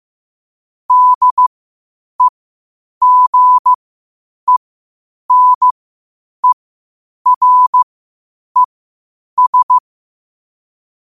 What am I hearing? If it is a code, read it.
Morse code, heard as a keyed tone, about 15 wpm, spaced as for 8 wpm, 1000 Hz. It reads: DEGENERES